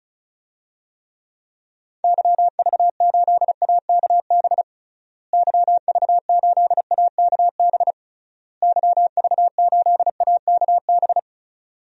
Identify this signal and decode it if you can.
Morse code, heard as a keyed tone, about 36 wpm, 700 Hz. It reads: YV8AKB YV8AKB YV8AKB
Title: DXCC-100 Call Signs - 6 Characters - Repeated 3x - 35wpm